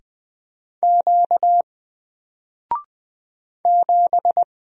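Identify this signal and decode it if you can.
Morse code, heard as a keyed tone, about 20 wpm, 700 Hz. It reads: Q 7
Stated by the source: Single letter and number Morse drill at 20 wpm, from N4LCD.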